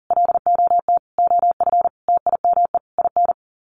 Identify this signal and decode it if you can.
Morse code, heard as a keyed tone, about 40 wpm, 700 Hz. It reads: LOT OF TIME IN
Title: Sets of 5 Words Encore2 40wpm